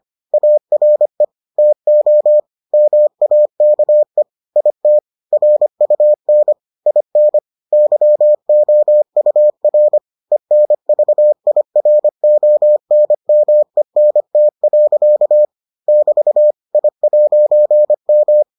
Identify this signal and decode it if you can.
Morse code, heard as a keyed tone, about 25 wpm, 600 Hz. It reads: ARE TO MAKE IT RUN IN YOUR ENVIRONMENT. <BT> I'M